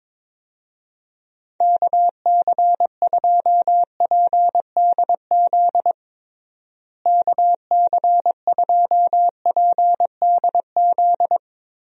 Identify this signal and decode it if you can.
Morse code, heard as a keyed tone, about 22 wpm, 700 Hz. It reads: KC2PDZ KC2PDZ
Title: USA Call Signs - 6 Characters - Repeated 3x - 22wpm